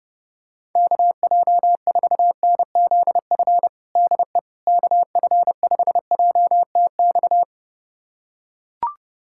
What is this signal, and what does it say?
Morse code, heard as a keyed tone, about 32 wpm, 700 Hz. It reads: KJ4NZF DE KF5JTX
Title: USA Callsign DE Callsign - 6 Characters - 30wpm